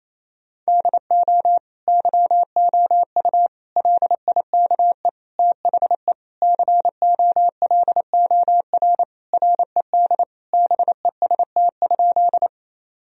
Morse code, heard as a keyed tone, about 28 wpm, 700 Hz. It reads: DO YOU LIKE THE COLOR RED BEST?
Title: Sentences from Top 1000 Words - Review Part 2 - 28wpm